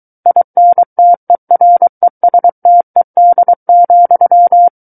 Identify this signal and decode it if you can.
Morse code, heard as a keyed tone, about 23 wpm, 700 Hz. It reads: INTERESTED,
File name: Book One / 14-Chapter 14 - 23 WPM